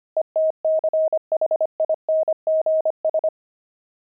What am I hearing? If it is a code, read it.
Morse code, heard as a keyed tone, about 25 wpm, 630 Hz. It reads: ETCHINGS
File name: pangram-fs-25-cs-25